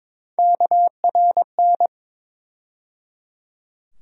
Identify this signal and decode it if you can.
Morse code, heard as a keyed tone, about 22 wpm, 700 Hz. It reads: KRN
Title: USA Call Sign Postfixes - 3 Letters 22wpm